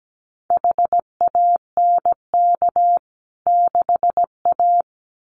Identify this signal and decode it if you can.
Morse code, heard as a keyed tone, about 17 wpm, 700 Hz. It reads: HANK 6A
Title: CWT Exchange 17wpm